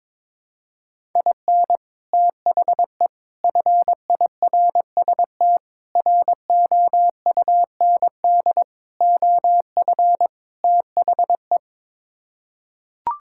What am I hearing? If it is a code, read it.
Morse code, heard as a keyed tone, about 21 wpm, 700 Hz. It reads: IN THE FIRST ROUND OF THE
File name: practice / 22wpm / Sets of 6 Words - 22wpm